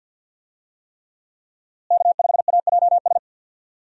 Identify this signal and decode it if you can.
Morse code, heard as a keyed tone, about 48 wpm, 700 Hz. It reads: K5AJS
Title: USA Call Signs - 5 Characters - Repeated 3x - 50wpm